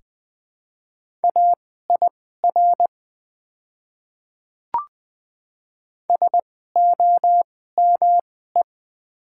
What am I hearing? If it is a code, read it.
Morse code, heard as a keyed tone, about 20 wpm, spaced as for 14 wpm, 700 Hz. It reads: AIR SOME
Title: 2x Character Spacing - Top 200 Words - 20wpm